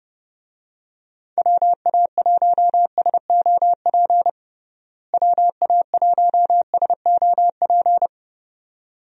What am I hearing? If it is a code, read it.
Morse code, heard as a keyed tone, about 32 wpm, 700 Hz. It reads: WA1SOP WA1SOP
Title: USA Call Signs - 6 Characters - Repeated 2x - 30wpm